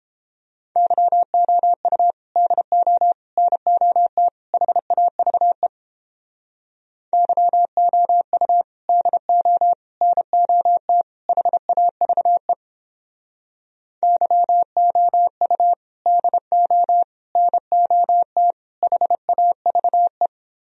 Morse code, heard as a keyed tone, about 29 wpm, 700 Hz. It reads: YOU DO NOT HAVE YOU DO NOT HAVE YOU DO NOT HAVE